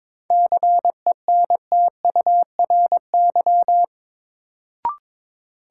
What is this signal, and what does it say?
Morse code, heard as a keyed tone, about 22 wpm, 700 Hz. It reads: CENTURY
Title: Sets of 5 Words - New Word Drill - 22wpm